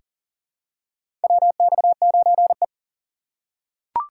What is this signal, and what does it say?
Morse code, heard as a keyed tone, about 40 wpm, 700 Hz. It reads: WX9E
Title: USA Call Signs - 4 Characters - 2x Character Spacing - 40wpm